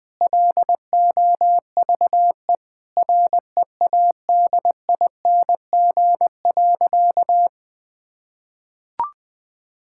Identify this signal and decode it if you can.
Morse code, heard as a keyed tone, about 20 wpm, 700 Hz. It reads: LOVE READING.
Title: Sentences with 7-Word Phrases - 20wpm